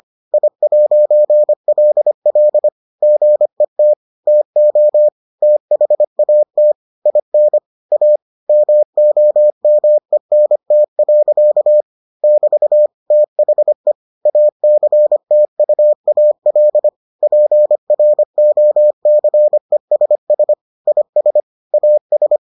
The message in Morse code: I'LL GET TO THAT IN A MOMENT. <BT> THE ACTUAL PROCESS IS AS

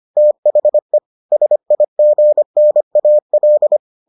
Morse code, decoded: THE SIGNAL

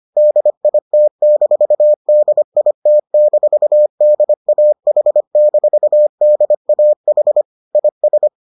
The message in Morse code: DIT-DIT-DAH-DAH IS

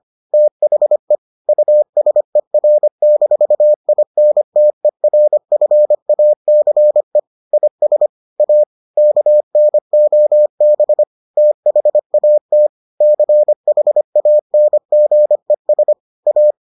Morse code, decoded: THE USER-INTERFACE IS A KNOB THAT CHANGES A

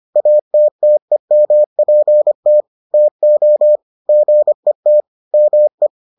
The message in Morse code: ATTEMPT TO GET ME

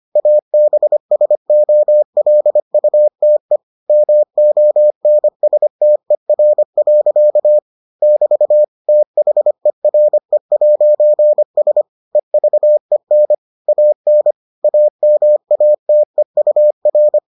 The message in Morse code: ABSOLUTE MONSTER. <BT> THERE'S EVEN AN AMATEUR

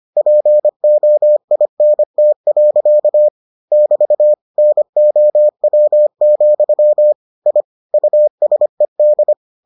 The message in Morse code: POINT. <BT> NOW, I USED

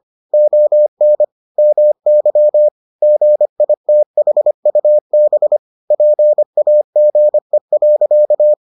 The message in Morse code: ON MY GITHUB PAGE.

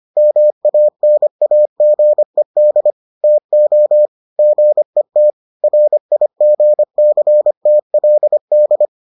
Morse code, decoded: MANAGED TO GET RIGCTLD

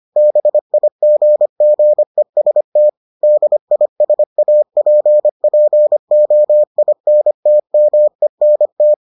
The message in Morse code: BIGGEST DISAPPOINTMENT

